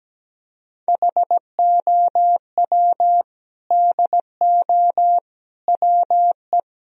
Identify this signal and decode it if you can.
Morse code, heard as a keyed tone, about 17 wpm, 700 Hz. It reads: HOW DO WE
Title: Sentences from Top 1000 Words - Review Part 1 - 17wpm